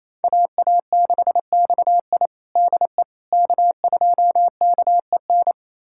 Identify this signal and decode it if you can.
Morse code, heard as a keyed tone, about 27 wpm, 700 Hz. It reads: AA6XI DE K2KEN
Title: USA Callsign DE Callsign - 5 Characters - 28wpm